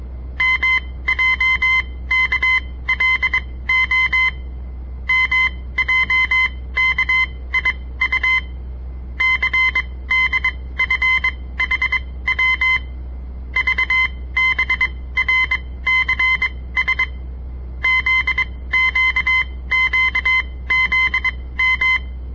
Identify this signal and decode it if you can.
Morse code, heard as a keyed tone, about 22 wpm, spaced as for 17 wpm, 2050 Hz. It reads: MJKLO MJKIU CDFHW VBRCS ZQQZM
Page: On the air Morse code practice from the Southern Montana Amatuer Radio Association; all code, no read back